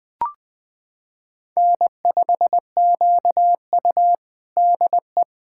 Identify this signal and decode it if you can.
Morse code, heard as a keyed tone, about 20 wpm, 700 Hz. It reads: N5QU DE
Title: USA Callsign DE Callsign - 4 Characters - 20wpm